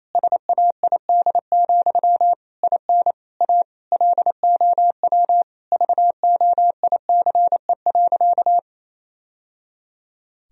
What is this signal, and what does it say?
Morse code, heard as a keyed tone, about 28 wpm, 700 Hz. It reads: SAID, IN A LOW VOICE.
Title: Sentences from Top 1000 Words - Review Part 1 - 28wpm